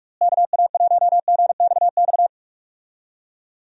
Morse code, heard as a keyed tone, about 46 wpm, 700 Hz. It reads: KA1GXX
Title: USA Call Signs - 6 Characters - Repeated 2x - 45wpm